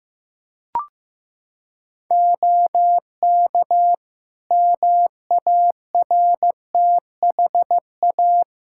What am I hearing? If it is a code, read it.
Morse code, heard as a keyed tone, about 15 wpm, 700 Hz. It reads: OK MARTHA
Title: Practice QSO - Exchange 3 - More Info - 15wpm